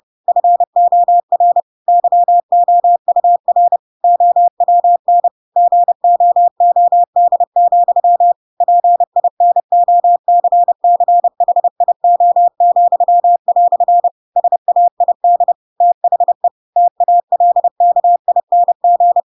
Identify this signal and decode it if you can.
Morse code, heard as a keyed tone, about 30 wpm, 700 Hz. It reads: FOR YOUR OWN GOOD, PINOCCHIO," SAID THE TALKING
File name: Pinocchio / Pinocchio ch4 30 30 ik1hge